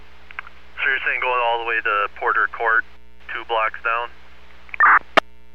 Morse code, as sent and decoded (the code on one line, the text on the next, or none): none